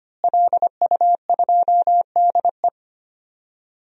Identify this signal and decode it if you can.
Morse code, heard as a keyed tone, about 25 wpm, 700 Hz. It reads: LU2DE
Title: DXCC-100 Call Signs - 5 Characters - Repeated 3x - 25wpm